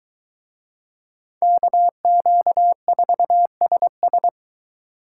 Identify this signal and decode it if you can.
Morse code, heard as a keyed tone, about 22 wpm, 700 Hz. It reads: KQ4SS